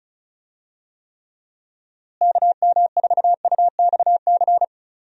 Morse code, heard as a keyed tone, about 34 wpm, 700 Hz. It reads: KM4UXC
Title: USA Call Signs - 6 Characters - Repeated 2x - 35wpm